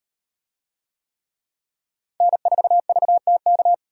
Morse code, heard as a keyed tone, about 38 wpm, 700 Hz. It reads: N4VTK